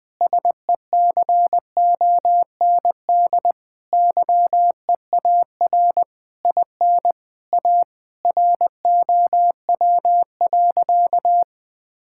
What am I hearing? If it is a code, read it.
Morse code, heard as a keyed tone, about 20 wpm, 700 Hz. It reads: SECOND YEAR IN A ROW.
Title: Sentences with 7-Word Phrases - 20wpm